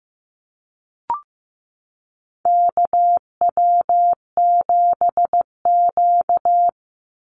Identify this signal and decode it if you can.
Morse code, heard as a keyed tone, about 15 wpm, 700 Hz. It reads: KW7Q
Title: Call Signs Repeated 3x - 15wpm